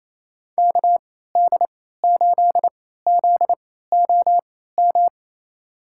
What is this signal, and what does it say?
Morse code, heard as a keyed tone, about 28 wpm, spaced as for 16 wpm, 700 Hz. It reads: KD8ZOM
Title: USA Call Signs - 3x Character Spacing - 28wpm